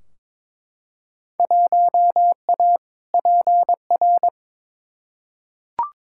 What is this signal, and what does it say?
Morse code, heard as a keyed tone, about 22 wpm, 700 Hz. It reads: 1A PR
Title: Field Day Exchanges Encore - 22wpm